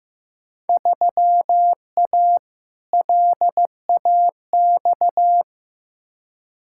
Morse code, heard as a keyed tone, about 16 wpm, 700 Hz. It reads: 3A LAX
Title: Field Day Exchanges - 15wpm